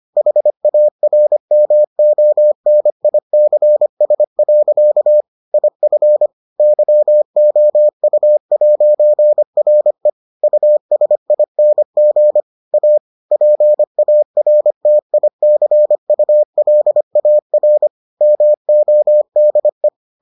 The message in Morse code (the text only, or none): HARMONICS. IF YOU'RE USING A PARTICULAR MODE